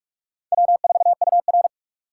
Morse code, heard as a keyed tone, about 44 wpm, 700 Hz. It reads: W4UR